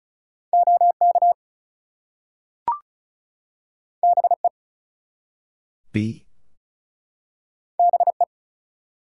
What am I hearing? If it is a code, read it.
Morse code, heard as a keyed tone, about 36 wpm, 700 Hz. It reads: OK BE BE
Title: Two Letter Words - 35wpm